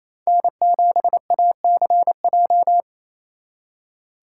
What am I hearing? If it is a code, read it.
Morse code, heard as a keyed tone, about 36 wpm, 700 Hz. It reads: N7ACJ